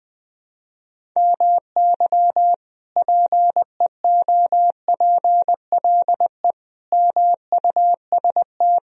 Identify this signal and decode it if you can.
Morse code, heard as a keyed tone, about 20 wpm, 700 Hz. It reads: MY PEOPLE MUST
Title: Sentences from Top 1000 Words - Review Part 3 - 20wpm